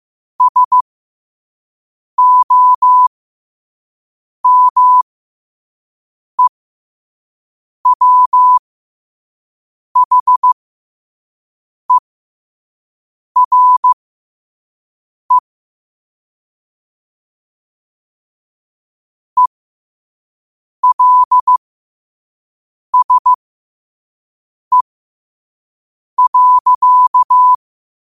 Morse code, heard as a keyed tone, about 15 wpm, spaced as for 5 wpm, 1000 Hz. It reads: SOMEWHERE ELSE.
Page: New quotes every day in morse code at 5 Words per minute.